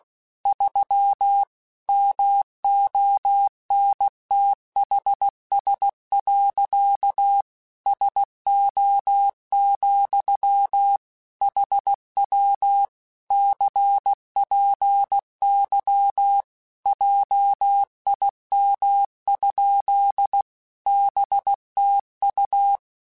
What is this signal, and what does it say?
Morse code, heard as a keyed tone, about 16 wpm, 800 Hz. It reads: 3 MONTHS. SO, HW CPY JIM? BTU
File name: QSOs / QSOs-mp3s / WD0BGP 1@15